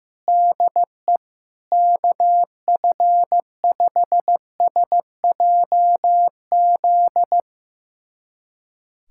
Morse code, heard as a keyed tone, about 15 wpm, 700 Hz. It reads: DE KF5SJZ
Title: USA Callsign DE Callsign - 6 Characters - 15wpm